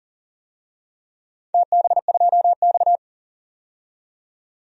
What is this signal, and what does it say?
Morse code, heard as a keyed tone, about 40 wpm, 700 Hz. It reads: TB2X
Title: DXCC-100 Call Signs - 4 Characters - 40wpm